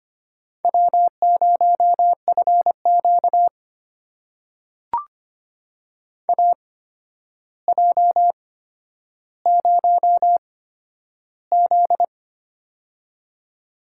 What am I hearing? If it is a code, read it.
Morse code, heard as a keyed tone, about 25 wpm, spaced as for 11 wpm, 700 Hz. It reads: W0FQ AJ0Z
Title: USA Call Signs - 4 Characters - 8x Character Spacing - 25wpm